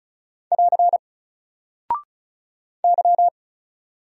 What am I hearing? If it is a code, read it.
Morse code, heard as a keyed tone, about 36 wpm, 700 Hz. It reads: <AR> Y
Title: Single Character (Letter:Number:Punctuation:Prosign) Alt 35wpm